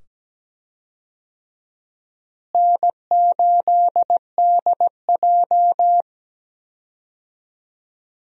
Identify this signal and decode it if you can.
Morse code, heard as a keyed tone, about 17 wpm, 700 Hz. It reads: N8DJ
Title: USA Call Signs - 4 Characters - Repeated 2x - 17wpm